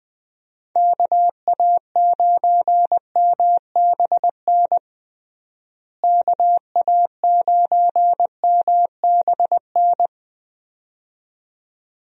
Morse code, heard as a keyed tone, about 20 wpm, 700 Hz. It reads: KA9MBN KA9MBN
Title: USA Call Signs - 6 Characters - Repeated 2x - 20wpm